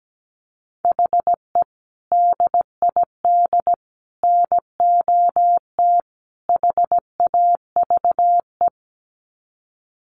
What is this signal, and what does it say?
Morse code, heard as a keyed tone, about 17 wpm, 700 Hz. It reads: HE DID NOT HAVE
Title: Sets of 4 Words 17wpm